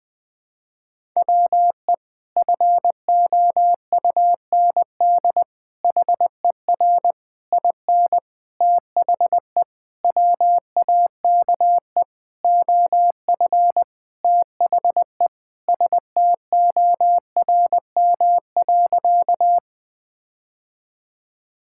Morse code, heard as a keyed tone, about 21 wpm, 700 Hz. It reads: WE FOUND HER IN THE WAKE OF THE STORM.
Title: Sentences with 4-Word Phrases - 20wpm